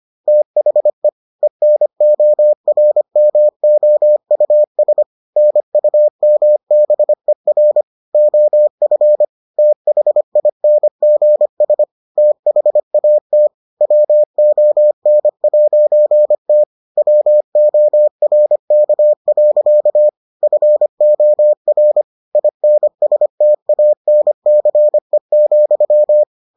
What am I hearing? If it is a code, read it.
Morse code, heard as a keyed tone, about 25 wpm, 600 Hz. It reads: THE ENORMOUS NUMBER OF THINGS THAT WON'T WORK. FOR INSTANCE,